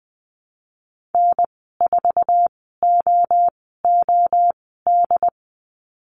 Morse code, heard as a keyed tone, about 20 wpm, spaced as for 15 wpm, 700 Hz. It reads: N4OOD
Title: 076 Lesson 13g - Letter L - Call Signs 20wpm